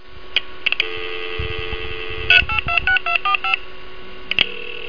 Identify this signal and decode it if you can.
DTMF, heard as telephone touch tones, tones of about 100 ms. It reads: 3826245
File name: MAIN2.Dxr / 00049 Sound modem / 00049 Sound modem